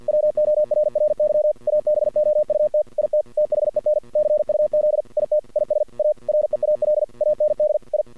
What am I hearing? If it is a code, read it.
Morse code, heard as a keyed tone, about 50 wpm, 610 Hz. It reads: WORRY ABOUT IT THEN CUZ IT IS A REAL AND I
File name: archived / CRSnet